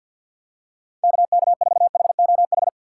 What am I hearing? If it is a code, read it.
Morse code, heard as a keyed tone, about 51 wpm, 700 Hz. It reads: KK4HOH
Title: USA Call Signs - 6 Characters - 3x Character Spacing - 50wpm